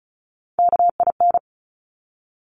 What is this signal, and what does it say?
Morse code, heard as a keyed tone, about 36 wpm, 700 Hz. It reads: KIN
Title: Top Three Letter Combinations in Words 35wpm